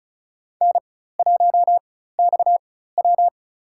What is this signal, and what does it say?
Morse code, heard as a keyed tone, about 36 wpm, spaced as for 16 wpm, 700 Hz. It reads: N1XW
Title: USA Call Signs - 4 Characters - 4x Character Spacing - 35wpm